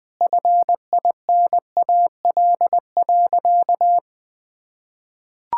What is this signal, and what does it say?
Morse code, heard as a keyed tone, about 20 wpm, 700 Hz. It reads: FINAL.
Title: Sentences from Top 1600 Words 20wpm